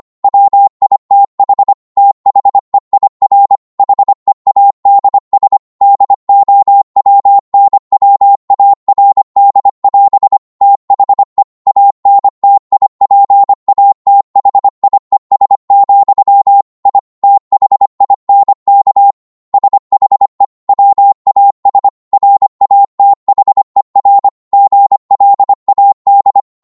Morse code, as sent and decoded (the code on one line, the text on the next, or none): .-- .. - .... / - .... .. .-. / .... .- -.. ... / -.. --- .-- -. .-- .- .-. -.. .-... / - .... / .- -. - .. .--. .- - .... .. ... --..-- / .. / - .... .. -. -.- / ... .... / .-- .- ... / .-. .- - .... .-. / --. .-.. .- -..
WITH THEIR HEADS DOWNWARD<AS> THE ANTIPATHIES, I THINK SHE WAS RATHER GLAD